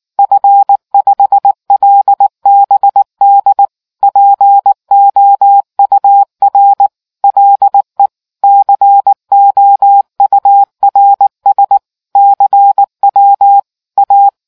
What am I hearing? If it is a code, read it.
Morse code, heard as a keyed tone, about 19 wpm, 800 Hz. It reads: F5LBD POUR LE COURS CW A